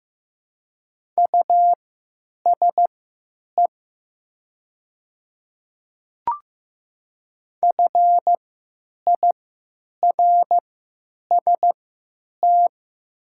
Morse code, heard as a keyed tone, about 15 wpm, spaced as for 9 wpm, 700 Hz. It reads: USE FIRST
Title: 3x Character Spacing - Top 100 Words - 15wpm